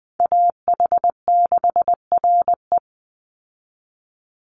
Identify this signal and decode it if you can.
Morse code, heard as a keyed tone, about 20 wpm, 700 Hz. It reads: AH6RE
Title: Call Signs Repeated 3x - 20wpm